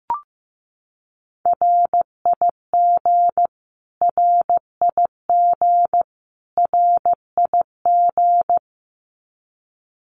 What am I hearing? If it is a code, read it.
Morse code, heard as a keyed tone, about 15 wpm, 700 Hz. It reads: RIG RIG RIG
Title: Abbreviations Repeated 3x - 15wpm